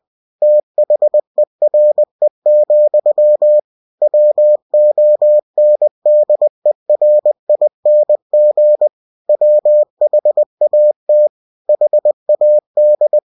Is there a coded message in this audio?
Yes: Morse code from 0.4 to 13.2 s, heard as a keyed tone, about 20 wpm, 600 Hz. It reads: THERE, WONDERING WHAT HAD